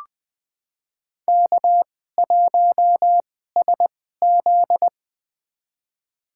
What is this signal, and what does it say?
Morse code, heard as a keyed tone, about 20 wpm, spaced as for 15 wpm, 700 Hz. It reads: K1SZ
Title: USA Call Signs - 4 Characters - 2x Character Spacing - 20wpm